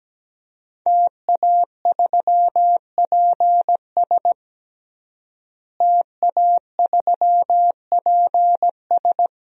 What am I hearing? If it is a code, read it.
Morse code, heard as a keyed tone, about 17 wpm, 700 Hz. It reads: TA3PS TA3PS
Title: DXCC-100 Call Signs - 5 Characters - Repeated 3x - 17wpm